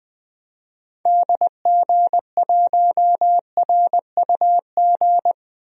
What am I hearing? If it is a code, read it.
Morse code, heard as a keyed tone, about 20 wpm, 700 Hz. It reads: DG1RUG
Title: DXCC-100 Call Signs - 6 Characters - 20wpm